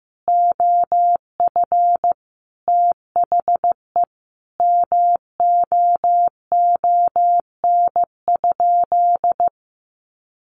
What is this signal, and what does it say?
Morse code, heard as a keyed tone, about 15 wpm, 700 Hz. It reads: OF THE MOON?
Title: Sentences from Top 1200 Words 15wpm